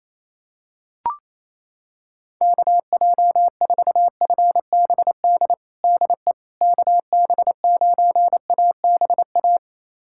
Morse code, heard as a keyed tone, about 28 wpm, 700 Hz. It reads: KJ4FBD DE KB9ABA
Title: USA Callsign DE Callsign - 6 Characters - 28wpm